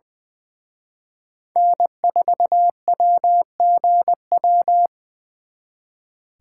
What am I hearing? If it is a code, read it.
Morse code, heard as a keyed tone, about 23 wpm, 700 Hz. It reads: N4WGW
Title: Speed Racing - USA 1x3 Call Signs 20wpm